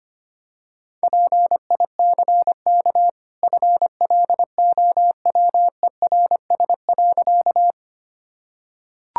Morse code, heard as a keyed tone, about 25 wpm, 700 Hz. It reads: PICK FLOWERS.
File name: Easy Sentences with Hints - 25wpm